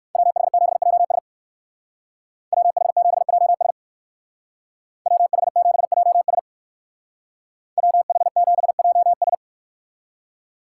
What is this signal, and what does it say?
Morse code, heard as a keyed tone, about 64 wpm, 700 Hz. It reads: WH7JS WH7JS WH7JS WH7JS